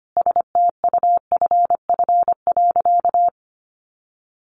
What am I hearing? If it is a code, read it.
Morse code, heard as a keyed tone, about 25 wpm, 700 Hz. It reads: STUFF.
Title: Sentences from Top 1400 Words 25wpm